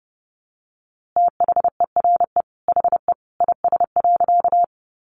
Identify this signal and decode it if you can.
Morse code, heard as a keyed tone, about 30 wpm, 700 Hz. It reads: THERE HE IS.
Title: A Sound Of Thunder 30wpm